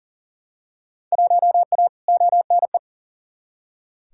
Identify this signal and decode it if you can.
Morse code, heard as a keyed tone, about 40 wpm, 700 Hz. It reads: 1A ONE
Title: Field Day Exchanges - 40wpm